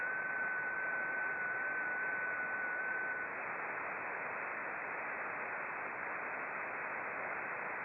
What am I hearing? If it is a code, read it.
FT8.
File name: contests / sept-vhf-k5tr- -0300